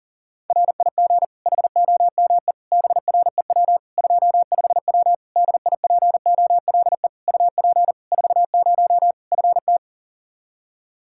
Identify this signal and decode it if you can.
Morse code, heard as a keyed tone, about 40 wpm, 700 Hz. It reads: RIG HOME BREW 25W DIPOLE UP 40 FT
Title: K1USN SST QSO Practice 40wpm